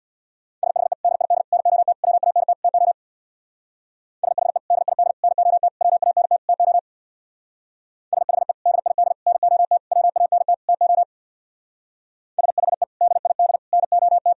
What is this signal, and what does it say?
Morse code, heard as a keyed tone, about 57 wpm, 690 Hz. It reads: SHE DID NOT WANT TO SHE DID NOT WANT TO SHE DID NOT WANT TO SHE DID NOT
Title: Speed-Racing - Sets of 5 Words 50wpm